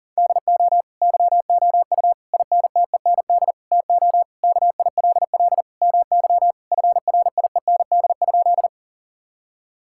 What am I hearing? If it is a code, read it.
Morse code, heard as a keyed tone, about 40 wpm, 700 Hz. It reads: DO YOU INTEND TO KILL MY FRIEND?
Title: Sentences from Top 1600 Words 40wpm